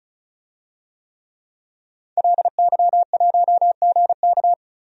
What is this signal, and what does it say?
Morse code, heard as a keyed tone, about 36 wpm, 700 Hz. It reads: LY1GK
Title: DXCC-100 Call Signs - 5 Characters - 35wpm